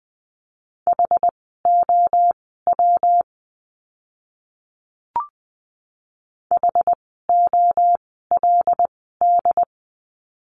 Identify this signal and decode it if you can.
Morse code, heard as a keyed tone, about 20 wpm, spaced as for 14 wpm, 700 Hz. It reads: HOW HOLD
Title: 146 Lesson 24f - Letter y - Words 20wpm